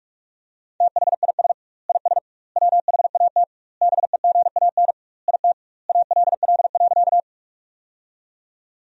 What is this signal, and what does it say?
Morse code, heard as a keyed tone, about 46 wpm, 700 Hz. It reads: THIS IS WHAT BEGAN IT ALL.
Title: 2x Word-Spacing - Sentences from Top 300 Words - 45wpm